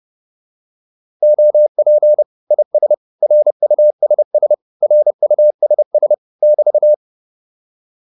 Morse code, heard as a keyed tone, about 30 wpm, 600 Hz. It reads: OP IS RUSS RUSS <BT>
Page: advanced-practice-files